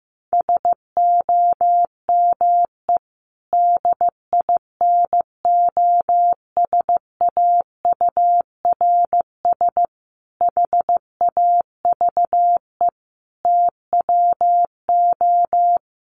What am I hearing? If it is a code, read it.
Morse code, heard as a keyed tone, about 15 wpm, 700 Hz. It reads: SOME DINOSAURS HAVE TWO
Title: A Sound Of Thunder 15wpm